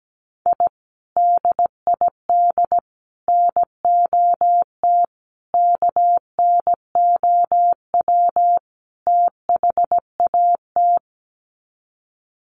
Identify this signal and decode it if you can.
Morse code, heard as a keyed tone, about 17 wpm, 700 Hz. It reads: I DID NOT KNOW THAT
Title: Sets of 5 Words 17wpm